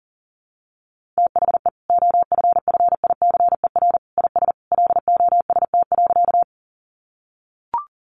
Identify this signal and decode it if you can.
Morse code, heard as a keyed tone, about 37 wpm, 700 Hz. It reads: THE OFFICER IS LOST.
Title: Sentences from Top 1400 Words 40wpm